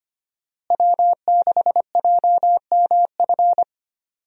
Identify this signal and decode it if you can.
Morse code, heard as a keyed tone, about 25 wpm, 700 Hz. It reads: W6JMF
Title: USA Call Signs - 5 Characters - Repeated 3x - 25wpm